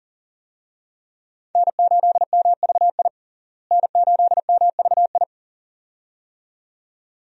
Morse code, heard as a keyed tone, about 37 wpm, 700 Hz. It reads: N8MVI N8MVI
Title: USA Call Signs - 5 Characters - Repeated 2x - 40wpm